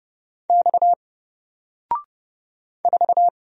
Morse code, heard as a keyed tone, about 31 wpm, 700 Hz. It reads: X 4
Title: Single Letter-Number 30wpm